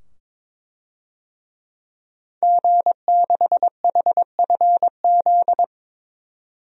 Morse code, heard as a keyed tone, about 22 wpm, 700 Hz. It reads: G6HFZ
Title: DXCC-100 Call Signs - 5 Characters - 22wpm